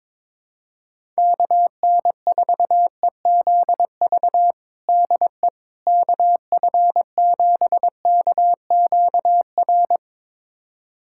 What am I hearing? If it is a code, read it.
Morse code, heard as a keyed tone, about 21 wpm, 700 Hz. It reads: KN4EZV DE KF7KQR
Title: USA Callsign DE Callsign 22wpm